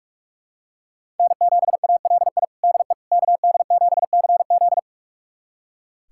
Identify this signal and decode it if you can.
Morse code, heard as a keyed tone, about 46 wpm, 700 Hz. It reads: N7ALI DE KD7CZ
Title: USA Callsign DE Callsign - 5 Characters - 45wpm